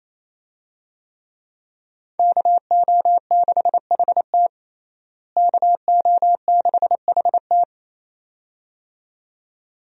Morse code, heard as a keyed tone, about 28 wpm, 700 Hz. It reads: KO6HT KO6HT
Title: USA Call Signs - 5 Characters - Repeated 2x - 28wpm